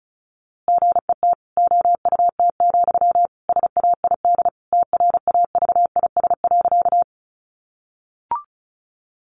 Morse code, heard as a keyed tone, about 34 wpm, 700 Hz. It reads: GET OUT, SAID TRAVIS.
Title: A Sound Of Thunder 35wpm